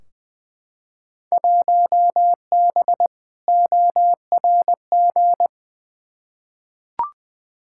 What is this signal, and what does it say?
Morse code, heard as a keyed tone, about 21 wpm, 700 Hz. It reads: 1B ORG
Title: Field Day Exchanges Encore - 20wpm